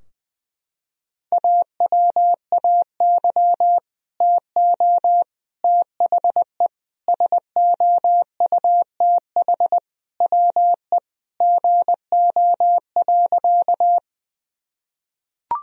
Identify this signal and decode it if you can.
Morse code, heard as a keyed tone, about 20 wpm, 700 Hz. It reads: AWAY TO THE SOUTH WE GO.
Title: Sentences from Top 1000 Words - Review Part 2 - 20wpm